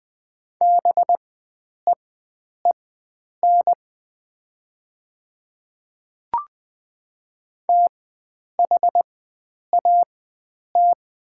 Morse code, heard as a keyed tone, about 20 wpm, spaced as for 9 wpm, 700 Hz. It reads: BEEN THAT
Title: 4x Character Spacing - Top 100 Words - 20wpm